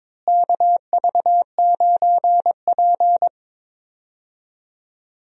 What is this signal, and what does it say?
Morse code, heard as a keyed tone, about 22 wpm, 700 Hz. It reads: KV9P
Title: USA Call Signs - 4 Characters - Repeated 3x - 22wpm